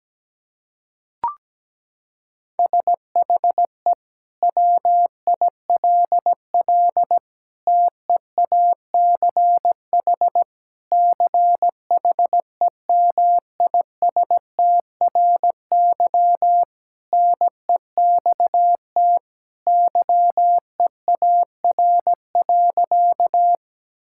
Morse code, heard as a keyed tone, about 17 wpm, 700 Hz. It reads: SHE WILL TEACH CHEMISTRY NEXT YEAR.
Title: Sentences from Top 1000 Words - Review Part 6 - 17wpm